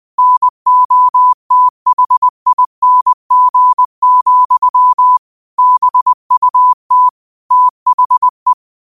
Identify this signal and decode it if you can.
Morse code, heard as a keyed tone, about 20 wpm, 1000 Hz. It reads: NOTHING, BUT THE
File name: qotdfiles / 20wpm